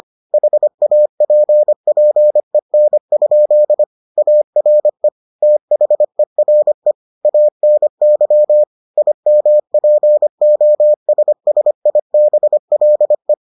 Morse code, heard as a keyed tone, about 25 wpm, 600 Hz. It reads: HAPPEN? ARE THERE ANY IMPOSSIBLE